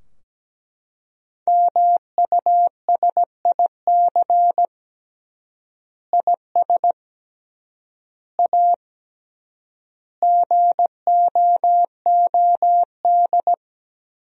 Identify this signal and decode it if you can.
Morse code, heard as a keyed tone, about 17 wpm, 700 Hz. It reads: MUSIC IS A GOOD
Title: 3x Word-Spacing - Sentences from Top 300 Words - 17wpm